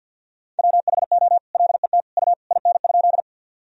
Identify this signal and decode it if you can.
Morse code, heard as a keyed tone, about 50 wpm, 700 Hz. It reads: WHO LET U IN?
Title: Sentences from Top 300 Words and Abbreviations 50wpm